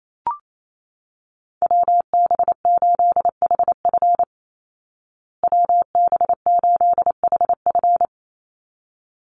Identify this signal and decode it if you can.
Morse code, heard as a keyed tone, about 28 wpm, 700 Hz. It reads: WB8HF WB8HF
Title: Call Signs Repeated 3x - 28wpm